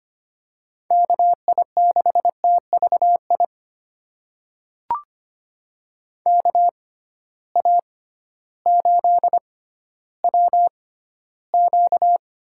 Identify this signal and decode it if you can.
Morse code, heard as a keyed tone, about 25 wpm, spaced as for 13 wpm, 700 Hz. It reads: KI6TVI KA8WQ
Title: USA Call Signs - 6 Characters - 6x Character Spacing - 25wpm